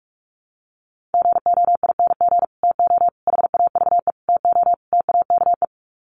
Morse code, heard as a keyed tone, about 44 wpm, 700 Hz. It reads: GOING TO HAVE TO TAKE
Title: Sets of 5 Words Encore 45wpm